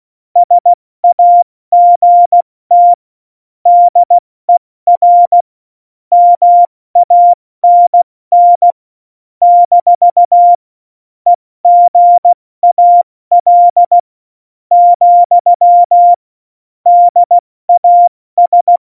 Morse code, heard as a keyed tone, about 16 wpm, 700 Hz. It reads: SAGT DER MANN - EGAL , DAS
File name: Morsekurs / Witze / Witz 13 March 2018, 70 F80